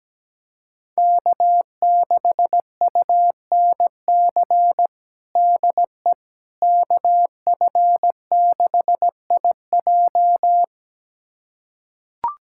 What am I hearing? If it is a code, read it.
Morse code, heard as a keyed tone, about 17 wpm, 700 Hz. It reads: K6UNC DE KF6IJ
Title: USA Callsign DE Callsign - 5 Characters - 17wpm